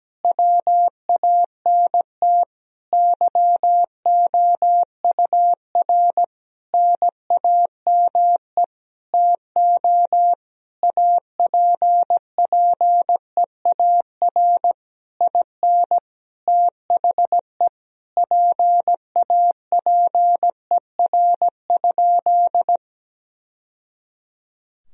Morse code, heard as a keyed tone, about 17 wpm, 700 Hz. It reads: WANT YOUR NAME TO APPEAR IN THE PAPER?
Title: Sentences from Top 1000 Words - Review Part 2 - 17wpm